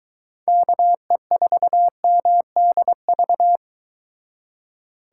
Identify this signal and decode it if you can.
Morse code, heard as a keyed tone, about 23 wpm, 700 Hz. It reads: KE4MDV